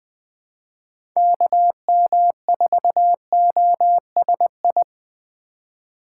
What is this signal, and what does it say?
Morse code, heard as a keyed tone, about 20 wpm, 700 Hz. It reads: KM4OSI
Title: USA Call Signs - 6 Characters - 4x Character Spacing - 20wpm